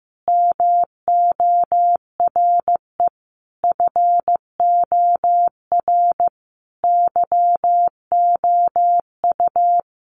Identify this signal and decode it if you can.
Morse code, heard as a keyed tone, about 15 wpm, 700 Hz. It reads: MORE FOR YOU
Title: Sentences from Top 1100 Words 15wpm